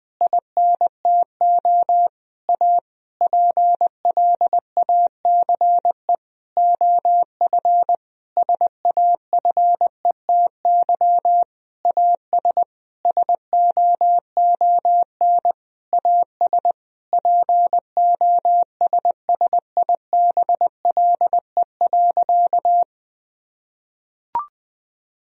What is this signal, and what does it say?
Morse code, heard as a keyed tone, about 20 wpm, 700 Hz. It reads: INTO A PLACE OF SAFETY AS SOON AS POSSIBLE.
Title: Sentences from Top 1000 Words - Review Part 4 - 20wpm